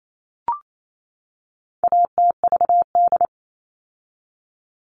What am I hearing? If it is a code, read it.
Morse code, heard as a keyed tone, about 28 wpm, 700 Hz. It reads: ATVD
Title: All Cut Numbers - 4-Digit 28wpm